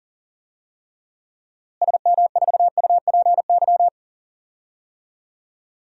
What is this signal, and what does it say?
Morse code, heard as a keyed tone, about 40 wpm, 700 Hz. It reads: SM4UPY
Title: DXCC-100 Call Signs - 6 Characters - Repeated 2x - 40wpm